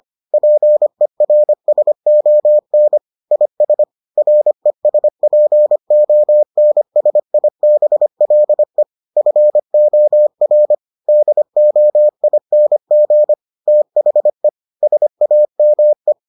Morse code, heard as a keyed tone, about 25 wpm, 600 Hz. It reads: PERSON IS RESPONSIBLE FOR DOING THE SAME